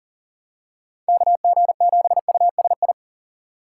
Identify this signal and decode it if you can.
Morse code, heard as a keyed tone, about 40 wpm, 700 Hz. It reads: KG7USI
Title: USA Call Signs - 6x Character Spacing - 40wpm